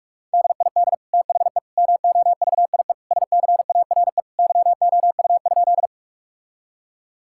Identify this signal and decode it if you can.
Morse code, heard as a keyed tone, about 46 wpm, 700 Hz. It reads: DID THE MOVIE SCARE YOU?
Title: Sentences from Top 1000 Words - Review Part 5 - 45wpm